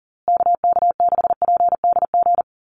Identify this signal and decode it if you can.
Morse code, heard as a keyed tone, about 40 wpm, 700 Hz. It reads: KK6PDG
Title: USA Call Signs Repeated 3x 40wpm